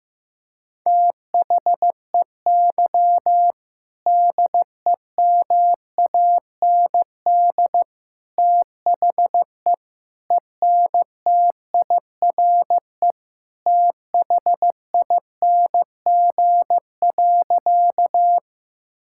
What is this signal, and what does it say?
Morse code, heard as a keyed tone, about 15 wpm, 700 Hz. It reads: THEY DEMAND THE ENTIRE THING.
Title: Sentences from Top 1000 Words - Review Part 5 - 15wpm